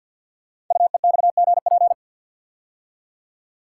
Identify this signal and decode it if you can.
Morse code, heard as a keyed tone, about 48 wpm, 700 Hz. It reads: UEXGP
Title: Code Groups - 5 Letters - 50wpm